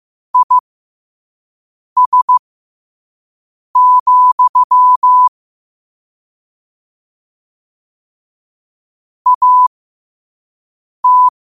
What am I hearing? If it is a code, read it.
Morse code, heard as a keyed tone, about 15 wpm, spaced as for 5 wpm, 1000 Hz. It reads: IS, AT
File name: qotdfiles / 5wpm